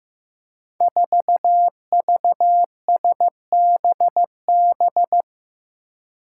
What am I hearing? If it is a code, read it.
Morse code, heard as a keyed tone, about 15 wpm, 700 Hz. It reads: 4VSBB
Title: Five Challenging Letter-Number Combinations 15wpm